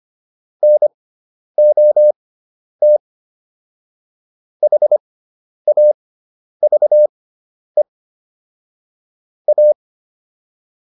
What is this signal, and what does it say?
Morse code, heard as a keyed tone, about 25 wpm, spaced as for 10 wpm, 600 Hz. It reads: NOT HAVE A